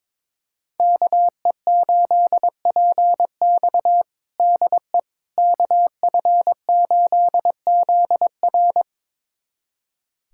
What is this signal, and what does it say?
Morse code, heard as a keyed tone, about 22 wpm, 700 Hz. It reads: KE8PX DE KF8ZR
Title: USA Callsign DE Callsign - 5 Characters - 22wpm